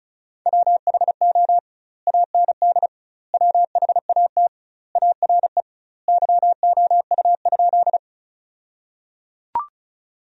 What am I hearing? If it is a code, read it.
Morse code, heard as a keyed tone, about 34 wpm, 700 Hz. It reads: WHO AND WHAT ARE YOU?
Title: 2x Word-Spacing - Sentences from Top 100 Words - 35wpm